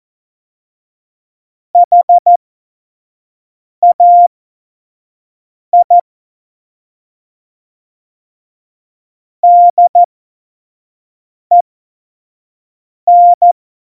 Morse code, heard as a keyed tone, about 14 wpm, spaced as for 5 wpm, 700 Hz. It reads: HAI DEN